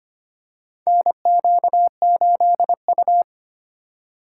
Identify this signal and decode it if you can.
Morse code, heard as a keyed tone, about 25 wpm, 700 Hz. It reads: NQ8U
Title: USA Call Signs - 4 Characters - 4x Character Spacing - 25wpm